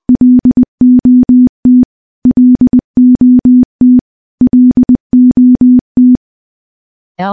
Morse code, heard as a keyed tone, about 20 wpm, 250 Hz. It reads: LOT LOT LOT
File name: morse-training-20WPM